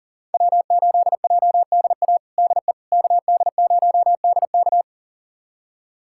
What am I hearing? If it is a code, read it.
Morse code, heard as a keyed tone, about 40 wpm, 700 Hz. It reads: W8JDA DE KD0DK
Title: USA Callsign DE Callsign - 5 Characters - 40wpm